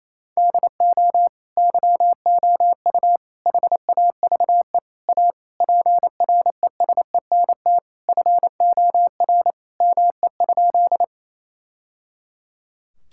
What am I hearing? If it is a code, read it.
Morse code, heard as a keyed tone, about 28 wpm, 700 Hz. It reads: DO YOU HAVE A PRESENT FOR ME?
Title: Sentences from Top 1000 Words - Review Part 5 - 28wpm